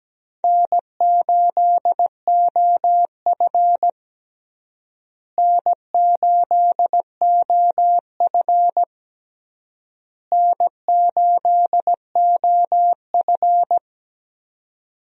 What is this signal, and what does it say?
Morse code, heard as a keyed tone, about 17 wpm, 700 Hz. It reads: N8OF N8OF N8OF
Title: USA Call Signs - 4 Characters - Repeated 3x - 17wpm